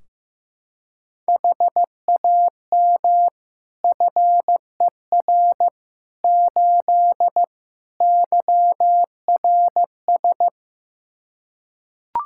Morse code, heard as a keyed tone, about 15 wpm, 700 Hz. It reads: HAM FER 8 YRS
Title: Practice QSO - Exchange 3 - More Info - 15wpm